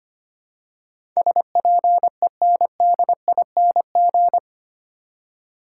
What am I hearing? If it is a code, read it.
Morse code, heard as a keyed tone, about 25 wpm, 700 Hz. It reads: SPENDING
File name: 8 Letter Words - First 3 Letters Given - 25wpm